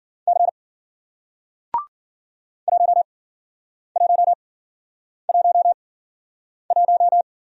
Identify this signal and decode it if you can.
Morse code, heard as a keyed tone, about 50 wpm, spaced as for 9 wpm, 700 Hz. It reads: C 1111
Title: Speed-Racing Single Letter-Number 40wpm